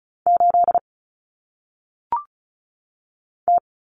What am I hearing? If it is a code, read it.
Morse code, heard as a keyed tone, about 36 wpm, 700 Hz. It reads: 8 T
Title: Common Cut Numbers - 1-Digit 35wpm